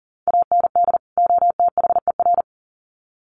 Morse code, heard as a keyed tone, about 40 wpm, 700 Hz. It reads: AND OTHER
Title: Sets of 2 Words 40wpm